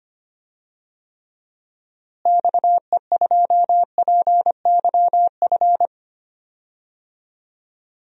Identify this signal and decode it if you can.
Morse code, heard as a keyed tone, about 25 wpm, 700 Hz. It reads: XE2PYF
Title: DXCC-100 Call Signs - 6 Characters - Repeated 2x - 25wpm